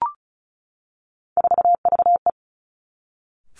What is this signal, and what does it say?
Morse code, heard as a keyed tone, about 36 wpm, 700 Hz. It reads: 4VE
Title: Three Challenging Letter-Number Combinations 35wpm